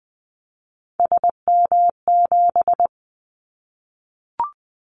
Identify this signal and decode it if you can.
Morse code, heard as a keyed tone, about 20 wpm, 700 Hz. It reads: SM7
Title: Callsign Prefixes for Top 25 DXCC - 20wpm